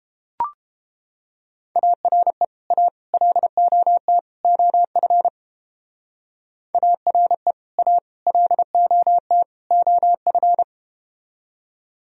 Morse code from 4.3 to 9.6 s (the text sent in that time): OF ARE A LOT